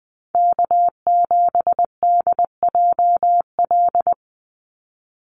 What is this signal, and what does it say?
Morse code, heard as a keyed tone, about 20 wpm, 700 Hz. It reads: K7DJL
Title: NA SOTA Chaser Callsigns 20wpm